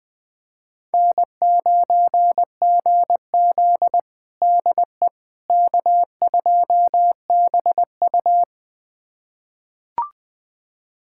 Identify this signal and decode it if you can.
Morse code, heard as a keyed tone, about 21 wpm, 700 Hz. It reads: N9GZ DE K2BU
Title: USA Callsign DE Callsign - 4 Characters - 20wpm